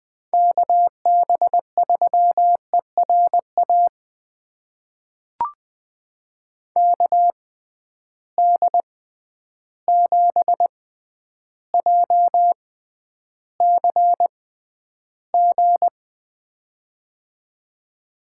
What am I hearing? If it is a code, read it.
Morse code, heard as a keyed tone, about 20 wpm, spaced as for 10 wpm, 700 Hz. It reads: KB3ERA KD7JCG
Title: USA Call Signs - 6x Character Spacing - 20wpm